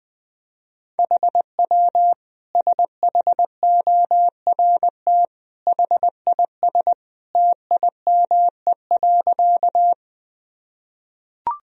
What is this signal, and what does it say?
Morse code, heard as a keyed tone, about 20 wpm, 700 Hz. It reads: HW SHORT HIS TIME.
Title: Sentences from Top 400 Words and Abbreviations 20wpm